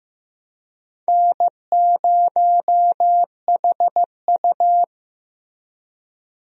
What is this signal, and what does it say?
Morse code, heard as a keyed tone, about 15 wpm, 700 Hz. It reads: N0HU
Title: USA Call Signs - 4 Characters - Repeated 3x - 15wpm